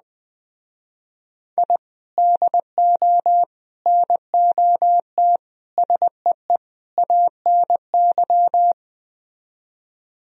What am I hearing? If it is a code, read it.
Morse code, heard as a keyed tone, about 20 wpm, 700 Hz. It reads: I DO NOT SEE ANY
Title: Speed-Racing - Sets of 5 Words 17wpm